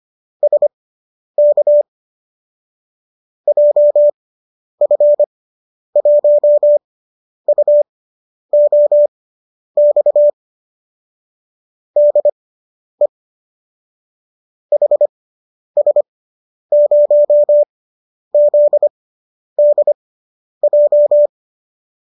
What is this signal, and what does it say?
Morse code, heard as a keyed tone, about 25 wpm, spaced as for 10 wpm, 590 Hz. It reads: SK JF1UOX DE HS0ZDJ